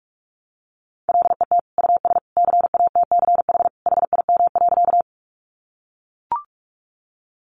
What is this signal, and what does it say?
Morse code, heard as a keyed tone, about 44 wpm, 700 Hz. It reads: LET US CATCH HIM.
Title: Sentences from Top 1500 Words 45wpm